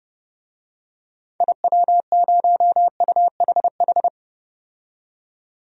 Morse code, heard as a keyed tone, about 30 wpm, 700 Hz. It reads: IW0UHH
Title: DXCC-100 Call Signs - 6 Characters - 30wpm